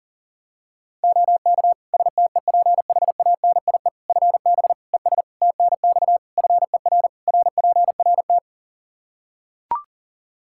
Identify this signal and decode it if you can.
Morse code, heard as a keyed tone, about 40 wpm, 700 Hz. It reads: OK STEPHANIE FB ES TNX FER RPRT
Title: Practice QSO - Exchange 2 - Info - 40wpm